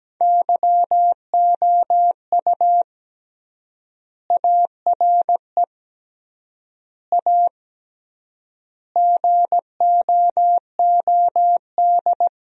Morse code, heard as a keyed tone, about 17 wpm, 700 Hz. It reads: YOU ARE A GOOD
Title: 3x Word-Spacing - Sentences from Top 200 Words - 17wpm